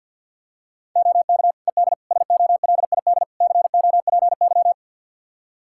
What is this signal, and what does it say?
Morse code, heard as a keyed tone, about 50 wpm, 700 Hz. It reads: OK ED SOLID COPY
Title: Practice QSO - Exchange 3 - More Info - 50wpm